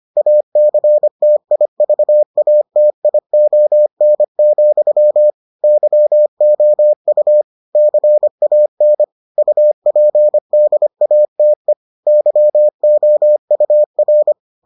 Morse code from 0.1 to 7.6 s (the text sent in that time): ACTIVATION, YOU